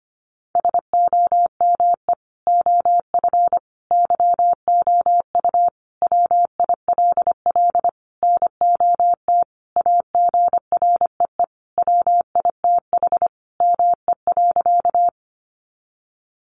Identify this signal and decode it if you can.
Morse code, heard as a keyed tone, about 25 wpm, 700 Hz. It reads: SOME OF YOU WILL NOT AGREE WITH ME.
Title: Sentences from Top 1100 Words 25wpm